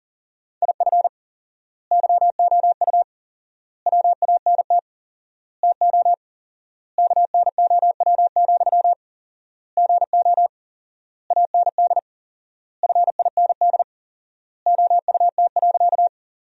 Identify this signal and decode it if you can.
Morse code, heard as a keyed tone, about 40 wpm, 700 Hz. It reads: IF YOU WANT TO KNOW, GO AND FIND OUT.
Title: 4x Word-Spacing - Sentences from Top 200 Words - 40wpm